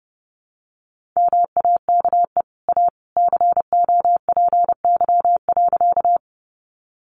Morse code, heard as a keyed tone, about 30 wpm, 700 Hz. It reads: MAKE A COPY.
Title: Sentences from Top 1400 Words 30wpm